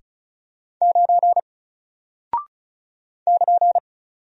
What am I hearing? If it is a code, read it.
Morse code, heard as a keyed tone, about 36 wpm, 700 Hz. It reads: 9 <KN>
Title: Single Character (Letter:Number:Punctuation:Prosign) 35wpm